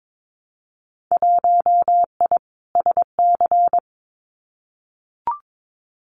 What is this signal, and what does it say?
Morse code, heard as a keyed tone, about 22 wpm, 700 Hz. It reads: 1I SC
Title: Winter Field Day Exchanges 22wpm